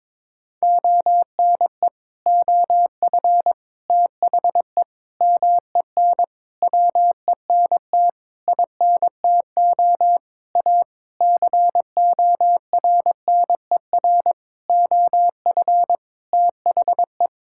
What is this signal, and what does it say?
Morse code, heard as a keyed tone, about 22 wpm, 700 Hz. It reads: ONE OF THE MEN WENT INTO A CORNER OF THE
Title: Sentences from Top 1600 Words 22wpm